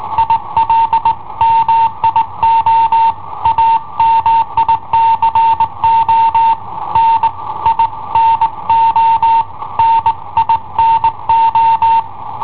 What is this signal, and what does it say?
Morse code, heard as a keyed tone, about 18 wpm, 910 Hz. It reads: IL MIO AMICO N I NO NINO
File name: i7ohp / cassetta i7ohp B